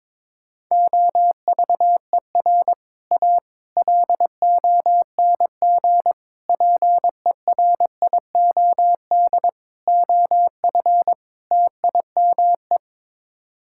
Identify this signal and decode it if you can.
Morse code, heard as a keyed tone, about 22 wpm, 700 Hz. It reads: OVER A LONG PERIOD OF TIME
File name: practice / 22wpm / Sets of 6 Words - 22wpm